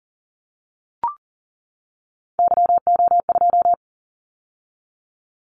Morse code, heard as a keyed tone, about 38 wpm, 700 Hz. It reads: YO2
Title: Callsign Prefixes for Top 25 DXCC - 40wpm